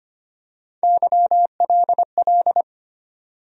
Morse code, heard as a keyed tone, about 25 wpm, 700 Hz. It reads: YLL
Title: USA Call Sign Postfixes - 3 Letters 25wpm